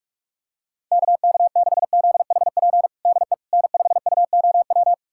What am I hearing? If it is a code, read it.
Morse code, heard as a keyed tone, about 46 wpm, 700 Hz. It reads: KK6ZHP DE N5UOW
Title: USA Callsign DE Callsign 45wpm